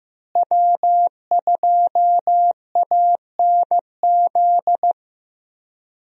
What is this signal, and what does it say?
Morse code, heard as a keyed tone, about 15 wpm, 700 Hz. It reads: W2ANZ
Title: USA Call Signs - 5 Characters - Repeated 3x - 15wpm